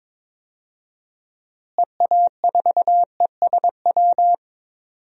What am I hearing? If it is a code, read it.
Morse code, heard as a keyed tone, about 22 wpm, 700 Hz. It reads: EA4ESW
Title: DXCC-100 Call Signs - 6 Characters - Repeated 2x - 22wpm